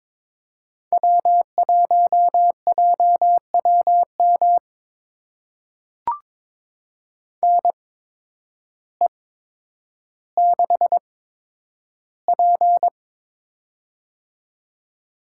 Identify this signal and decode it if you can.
Morse code, heard as a keyed tone, about 23 wpm, spaced as for 10 wpm, 700 Hz. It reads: W1JWM NE6P
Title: USA Call Signs - 8x Character Spacing - 22wpm